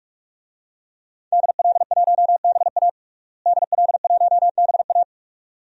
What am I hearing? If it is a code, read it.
Morse code, heard as a keyed tone, about 42 wpm, 700 Hz. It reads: DL1BA DL1BA
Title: DXCC-100 Call Signs - 5 Characters - Repeated 2x - 45wpm